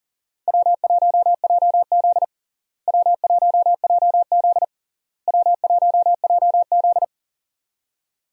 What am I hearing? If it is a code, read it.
Morse code, heard as a keyed tone, about 40 wpm, 700 Hz. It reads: W1JZ W1JZ W1JZ
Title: USA Call Signs - 4 Characters - Repeated 3x - 40wpm